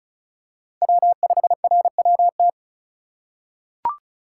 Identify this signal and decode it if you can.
Morse code, heard as a keyed tone, about 36 wpm, 700 Hz. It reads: W5RWT
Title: USA Call Signs - 5 Characters - 2x Character Spacing - 35wpm